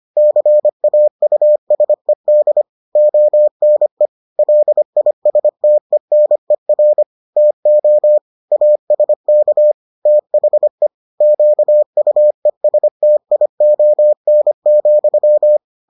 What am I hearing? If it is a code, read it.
Morse code, heard as a keyed tone, about 25 wpm, 600 Hz. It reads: CAUSED ONE LISTENER TO ASK THE QUESTION,